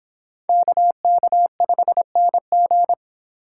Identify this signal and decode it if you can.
Morse code, heard as a keyed tone, about 25 wpm, 700 Hz. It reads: KK5NG